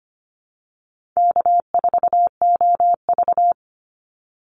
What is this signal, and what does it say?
Morse code, heard as a keyed tone, about 25 wpm, 700 Hz. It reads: K4OV
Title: Easy Call Signs Repeated 3x - 25wpm